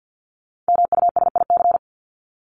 Morse code, heard as a keyed tone, about 50 wpm, 700 Hz. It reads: MUSIC
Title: Top 500 Words Repeated 50wpm